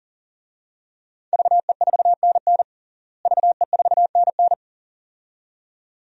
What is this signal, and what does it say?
Morse code, heard as a keyed tone, about 40 wpm, 700 Hz. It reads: VE4NN VE4NN
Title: DXCC-100 Call Signs - 5 Characters - Repeated 2x - 40wpm